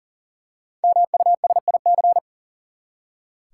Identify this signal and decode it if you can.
Morse code, heard as a keyed tone, about 40 wpm, 700 Hz. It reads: MUSIC
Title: Five Letter Words - 40wpm